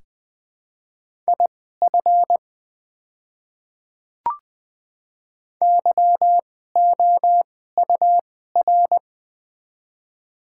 Morse code, heard as a keyed tone, about 20 wpm, spaced as for 14 wpm, 700 Hz. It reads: IF YOUR
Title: 2x Character Spacing - Top 300 Words - 20wpm